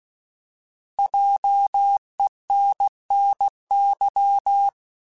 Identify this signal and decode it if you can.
Morse code, heard as a keyed tone, about 16 wpm, 800 Hz. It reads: JENNY
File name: Alphabetical / Alphabetical mp3s / Js @15